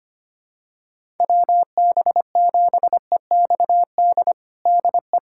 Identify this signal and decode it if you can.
Morse code, heard as a keyed tone, about 25 wpm, 700 Hz. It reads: WB7EXD DE
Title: USA Callsign DE Callsign 25wpm